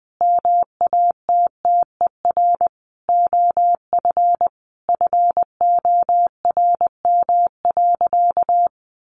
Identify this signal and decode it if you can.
Morse code, heard as a keyed tone, about 20 wpm, 700 Hz. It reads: MATTER OF FORM.
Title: Sentences from Top 1500 Words 20wpm